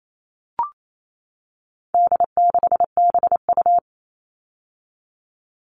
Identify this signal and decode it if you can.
Morse code, heard as a keyed tone, about 28 wpm, 700 Hz. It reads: D6BU
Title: All Cut Numbers - 4-Digit 28wpm